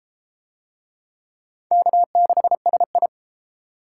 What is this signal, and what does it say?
Morse code, heard as a keyed tone, about 29 wpm, 700 Hz. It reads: K6SI